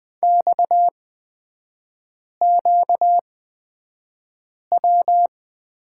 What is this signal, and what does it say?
Morse code, heard as a keyed tone, about 20 wpm, spaced as for 5 wpm, 700 Hz. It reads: XQW
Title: 236 Lesson 38e - Letter q - Three Characters 20wpm